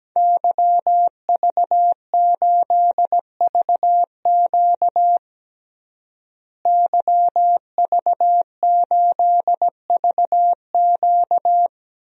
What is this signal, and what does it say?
Morse code, heard as a keyed tone, about 17 wpm, 700 Hz. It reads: YV8VQ YV8VQ
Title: DXCC-100 Call Signs - 5 Characters - Repeated 2x - 17wpm